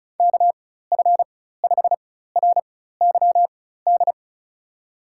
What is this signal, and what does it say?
Morse code, heard as a keyed tone, about 36 wpm, spaced as for 16 wpm, 700 Hz. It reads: KF5RYD
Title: USA Call Signs - 6 Characters - 4x Character Spacing - 35wpm